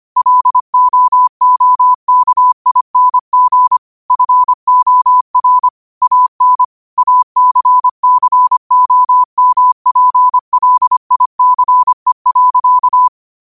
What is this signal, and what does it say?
Morse code, heard as a keyed tone, about 25 wpm, 1000 Hz. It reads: LOOKING FOR AN ACCOMPLICE.
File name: qotdfiles / 25wpm